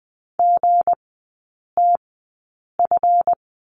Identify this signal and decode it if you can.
Morse code, heard as a keyed tone, about 20 wpm, spaced as for 8 wpm, 700 Hz. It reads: GTF
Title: 166 Lesson 27e - Number 7 - Three Characters 20wpm